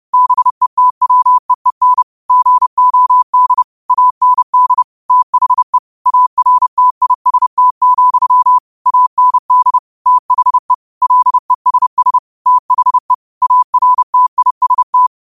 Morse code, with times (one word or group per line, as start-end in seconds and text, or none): 0.1-2.0 BETWEEN
2.3-3.6 GOD
3.9-4.8 AND
5.1-5.8 THE
6.1-8.6 ARTIST,
8.9-9.8 AND
10.1-10.7 THE
11.0-12.2 LESS
12.5-13.1 THE
13.4-15.1 ARTIST